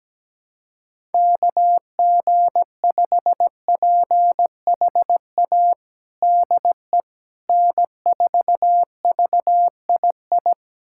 Morse code, heard as a keyed tone, about 17 wpm, 700 Hz. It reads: KG5PHA DE N4VII
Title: USA Callsign DE Callsign 17wpm